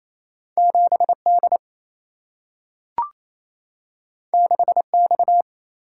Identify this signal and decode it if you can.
Morse code, heard as a keyed tone, about 28 wpm, 700 Hz. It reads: 7 D 6 X
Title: Single-Letter Single-Number 28wpm